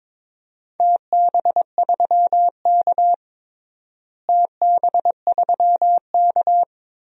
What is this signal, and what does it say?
Morse code, heard as a keyed tone, about 22 wpm, 700 Hz. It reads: TB3K TB3K
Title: DXCC-100 Call Signs - 4 Characters - Repeated 2x - 22wpm